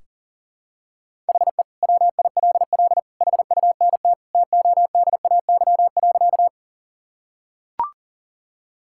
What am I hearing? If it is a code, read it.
Morse code, heard as a keyed tone, about 38 wpm, 700 Hz. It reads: HE WILL HUNT TODAY.
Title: Sentences from Top 1600 Words 40wpm